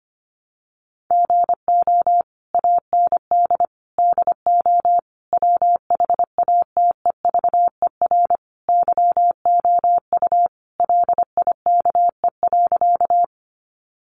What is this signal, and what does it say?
Morse code, heard as a keyed tone, about 25 wpm, 700 Hz. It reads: GO AND DO WHATEVER YOU LIKE.
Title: Sentences from Top 1100 Words 25wpm